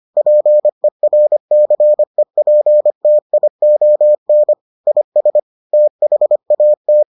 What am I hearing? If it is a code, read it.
Morse code, heard as a keyed tone, about 25 wpm, 600 Hz. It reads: PERCEPTION IS THAT